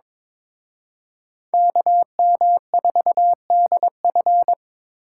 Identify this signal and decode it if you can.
Morse code, heard as a keyed tone, about 26 wpm, 700 Hz. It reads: KM4DF
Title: Speed Racing - USA 2x2 Call Signs 22wpm